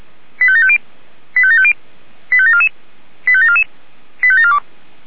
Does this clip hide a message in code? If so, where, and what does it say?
Yes: five-tone selective call (ZVEI) from 0.4 to 4.6 s, heard as quick runs of five tones, tones of about 70 ms. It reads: 85750 85750 85744 85744 85742